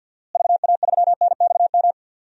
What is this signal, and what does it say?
Morse code, heard as a keyed tone, about 52 wpm, 700 Hz. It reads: VA3NXM
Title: DXCC-100 Call Signs - 6 Characters - Repeated 2x - 50wpm